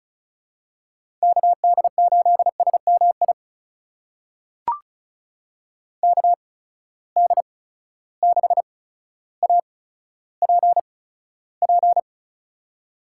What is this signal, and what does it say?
Morse code, heard as a keyed tone, about 36 wpm, spaced as for 15 wpm, 700 Hz. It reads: KD8SMI KD6APP
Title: USA Call Signs - 6 Characters - 8x Character Spacing - 35wpm